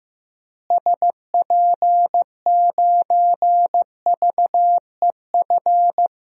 Morse code, heard as a keyed tone, about 15 wpm, 700 Hz. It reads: SP9VEF
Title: DXCC-100 Call Signs - 6 Characters - Repeated 2x - 15wpm